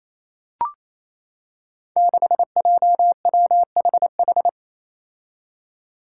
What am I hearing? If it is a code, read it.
Morse code, heard as a keyed tone, about 27 wpm, 700 Hz. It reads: 6JWHH
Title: Five Challenging Letter-Number Combinations 28wpm